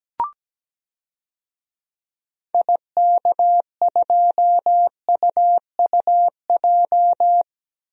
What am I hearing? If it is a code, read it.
Morse code, heard as a keyed tone, about 17 wpm, 700 Hz. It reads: IK2UUJ
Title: DXCC-100 Call Signs - 6 Characters - 17wpm